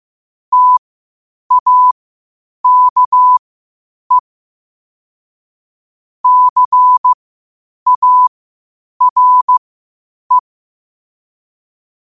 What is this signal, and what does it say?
Morse code, heard as a keyed tone, about 15 wpm, spaced as for 8 wpm, 1000 Hz. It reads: TAKE CARE
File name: qotdfiles / 8wpm